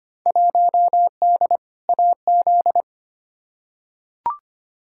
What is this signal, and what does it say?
Morse code, heard as a keyed tone, about 25 wpm, 700 Hz. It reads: 1D AZ
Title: Field Day Exchanges - 25wpm